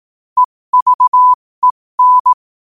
Morse code, heard as a keyed tone, about 18 wpm, spaced as for 15 wpm, 1000 Hz. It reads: EVEN
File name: qotdfiles / 15wpm